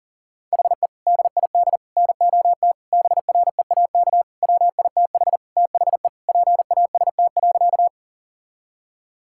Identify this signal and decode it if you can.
Morse code, heard as a keyed tone, about 40 wpm, 700 Hz. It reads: HE DID NOT BREAK WITH THE PAST.
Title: Sentences from Top 1000 Words - Review Part 6 - 40wpm